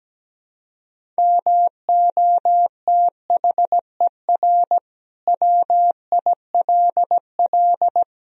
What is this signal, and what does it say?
Morse code, heard as a keyed tone, about 17 wpm, 700 Hz. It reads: MOTHER WILL
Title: Sentences from Top 1000 Words - Review Part 2 - 17wpm